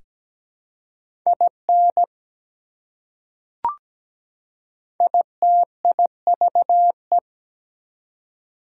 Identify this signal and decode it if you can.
Morse code, heard as a keyed tone, about 17 wpm, 700 Hz. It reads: IN ITIVE
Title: Prefixes and Suffixes 17wpm